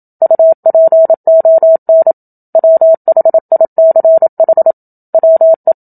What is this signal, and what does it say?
Morse code, heard as a keyed tone, about 27 wpm, 650 Hz. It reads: UPON WHICH WE
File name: Book Two / 01-Chapter 1 - 27 WPM